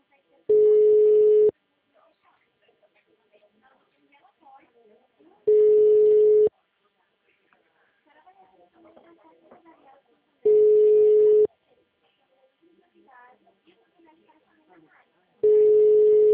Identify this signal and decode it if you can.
Ringback tone (Europe, 425 Hz), with speech.